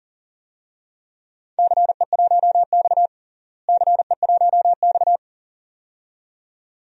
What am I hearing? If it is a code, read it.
Morse code, heard as a keyed tone, about 40 wpm, 700 Hz. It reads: CE1X CE1X
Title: DXCC-100 Call Signs - 4 Characters - Repeated 2x - 40wpm